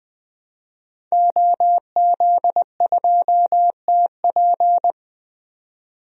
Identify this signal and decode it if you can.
Morse code, heard as a keyed tone, about 21 wpm, 700 Hz. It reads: OZ2TP
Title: DXCC-100 Call Signs - 5 Characters - 20wpm